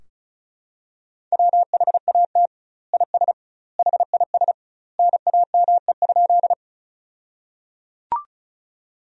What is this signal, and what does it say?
Morse code, heard as a keyed tone, about 36 wpm, 700 Hz. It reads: WHAT IS HIS NAME?
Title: 2x Word-Spacing - Sentences from Top 200 Words - 35wpm